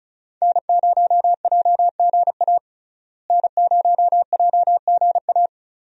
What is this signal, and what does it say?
Morse code, heard as a keyed tone, about 36 wpm, 700 Hz. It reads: N0JGA N0JGA
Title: USA Call Signs - 5 Characters - Repeated 3x - 35wpm